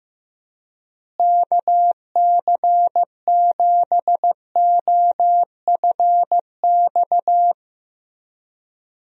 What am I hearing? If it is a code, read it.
Morse code, heard as a keyed tone, about 15 wpm, 700 Hz. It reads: KC7OFX
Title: USA Call Signs - 6 Characters - Repeated 3x - 15wpm